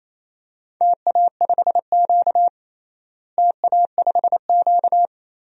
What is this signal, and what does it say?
Morse code, heard as a keyed tone, about 28 wpm, 700 Hz. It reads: TA5Q TA5Q
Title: DXCC-100 Call Signs - 4 Characters - Repeated 2x - 28wpm